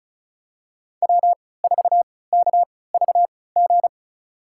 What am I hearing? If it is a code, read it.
Morse code, heard as a keyed tone, about 36 wpm, spaced as for 20 wpm, 700 Hz. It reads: W4KVG
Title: USA Call Signs - 5 Characters - 3x Character Spacing - 35wpm